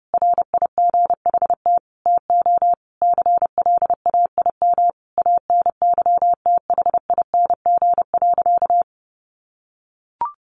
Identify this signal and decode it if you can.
Morse code, heard as a keyed tone, about 30 wpm, 700 Hz. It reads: RIGHT TO CLAIM ANYTHING.
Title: Sentences from Top 1500 Words 30wpm